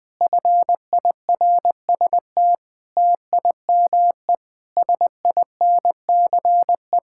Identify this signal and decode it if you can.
Morse code, heard as a keyed tone, about 21 wpm, 700 Hz. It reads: FIRST TIME SINCE
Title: Sentences with 5-Word Phrases - 20wpm